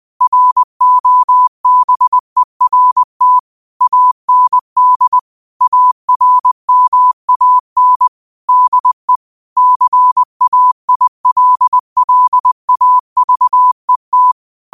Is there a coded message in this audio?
Yes: Morse code from 0.2 to 14.3 s, heard as a keyed tone, about 20 wpm, 1000 Hz. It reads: ROBERT AND ARMAN DE CAILLAVET